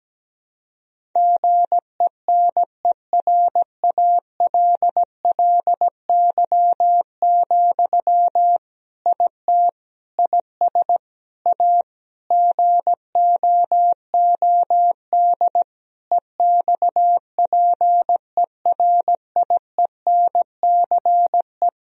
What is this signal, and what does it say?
Morse code, heard as a keyed tone, about 17 wpm, 700 Hz. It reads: GENERALLY, IT IS A GOOD EXPERIENCE
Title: Sentences from Top 1000 Words - Review Part 4 - 17wpm